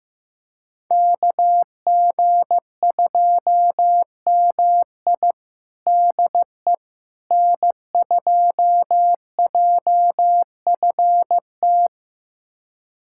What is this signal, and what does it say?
Morse code, heard as a keyed tone, about 15 wpm, 700 Hz. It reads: KG2MI DE N2JFT
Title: USA Callsign DE Callsign 15wpm